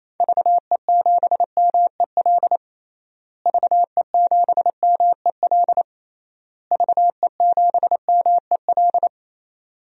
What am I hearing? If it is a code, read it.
Morse code, heard as a keyed tone, about 28 wpm, 700 Hz. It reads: VE7MEL VE7MEL VE7MEL
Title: DXCC-100 Call Signs - 6 Characters - Repeated 3x - 28wpm